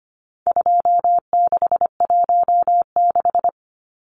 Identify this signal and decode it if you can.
Morse code, heard as a keyed tone, about 25 wpm, 700 Hz. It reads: 2616
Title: Numbers - 4-Digit 25wpm